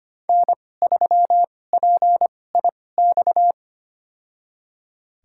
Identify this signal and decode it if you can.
Morse code, heard as a keyed tone, about 25 wpm, spaced as for 18 wpm, 700 Hz. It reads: N3PIX
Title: USA Call Signs - 2x Character Spacing - 25wpm